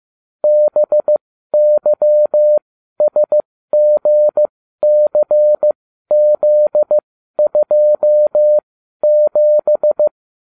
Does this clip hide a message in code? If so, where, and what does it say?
Yes: Morse code from 0.4 to 10.1 s, heard as a keyed tone, about 15 wpm, spaced as for 12 wpm, 600 Hz. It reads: BYSGCZ27